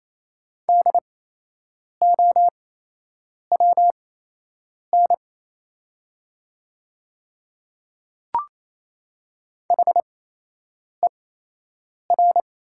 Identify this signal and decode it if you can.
Morse code, heard as a keyed tone, about 28 wpm, spaced as for 8 wpm, 700 Hz. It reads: DOWN HER
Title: 8x Character Spacing - Top 300 Words - 28wpm